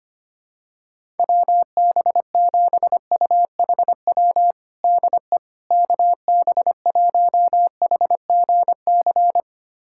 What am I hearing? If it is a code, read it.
Morse code, heard as a keyed tone, about 25 wpm, 700 Hz. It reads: WB7UHW DE KB1HGC
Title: USA Callsign DE Callsign 25wpm